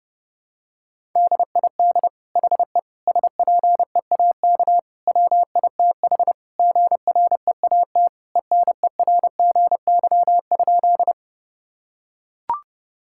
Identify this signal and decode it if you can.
Morse code, heard as a keyed tone, about 28 wpm, 700 Hz. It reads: DID HE SPEAK WITH GREAT ENERGY?
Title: Sentences from Top 1000 Words - Review Part 6 - 30wpm